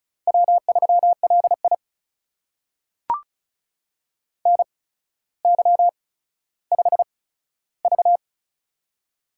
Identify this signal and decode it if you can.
Morse code, heard as a keyed tone, about 36 wpm, spaced as for 15 wpm, 700 Hz. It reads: W3LI NY5V
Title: USA Call Signs - 4 Characters - 8x Character Spacing - 35wpm